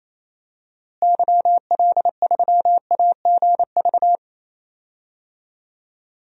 Morse code, heard as a keyed tone, about 28 wpm, 700 Hz. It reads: YL3AGV
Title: DXCC-100 Call Signs - 6 Characters - Repeated 2x - 28wpm